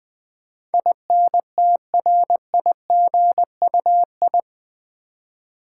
Morse code, heard as a keyed tone, about 20 wpm, 700 Hz. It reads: INTRIGUI
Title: Build A Word - 4 Letters At A Time - 10-Letter Words - 20wpm